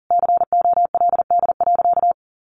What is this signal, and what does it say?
Morse code, heard as a keyed tone, about 40 wpm, 700 Hz. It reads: COLD.
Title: Sentences from Top 1100 Words 40wpm